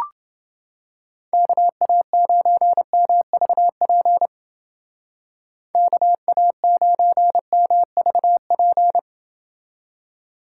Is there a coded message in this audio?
Yes: Morse code from 1.3 to 9.0 s, heard as a keyed tone, about 28 wpm, 700 Hz. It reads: KA9MVP KA9MVP